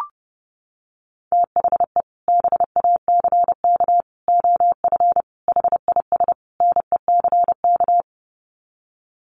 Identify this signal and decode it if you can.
Morse code, heard as a keyed tone, about 30 wpm, 700 Hz. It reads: THE BACK OF HIS NECK
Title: Sets of 5 Words Encore 30wpm